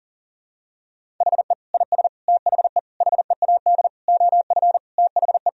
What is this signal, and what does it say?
Morse code, heard as a keyed tone, about 40 wpm, 700 Hz. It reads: HE IS THE HEAD OF THE
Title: Sentences from Top 1000 Words - Review Part 4 - 40wpm